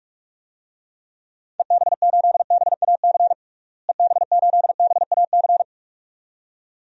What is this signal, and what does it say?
Morse code, heard as a keyed tone, about 47 wpm, 700 Hz. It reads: EB8BAC EB8BAC
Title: DXCC-100 Call Signs - 6 Characters - Repeated 2x - 45wpm